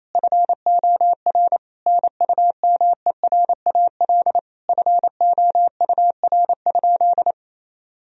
Morse code, heard as a keyed tone, about 28 wpm, 700 Hz. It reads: FOR NUMERAL FOUR?
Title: Sentences from Top 1000 Words - Review Part 2 - 28wpm